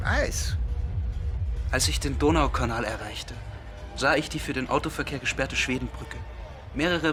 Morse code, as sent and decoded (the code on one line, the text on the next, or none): none